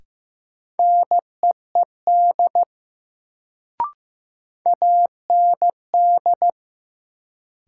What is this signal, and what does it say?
Morse code, heard as a keyed tone, about 15 wpm, 700 Hz. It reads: NEED AND
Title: Sets of 2 Words - New Word Drill - 15wpm